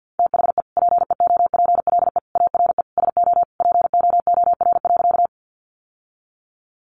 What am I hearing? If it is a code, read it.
Morse code, heard as a keyed tone, about 50 wpm, 700 Hz. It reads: THE PEOPLE ARE SO POOR.
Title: Sentences from Top 1000 Words 50wpm